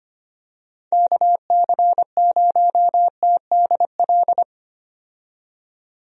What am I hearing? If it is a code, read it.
Morse code, heard as a keyed tone, about 25 wpm, 700 Hz. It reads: KC0TDL
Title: USA Call Signs - 6 Characters - 25wpm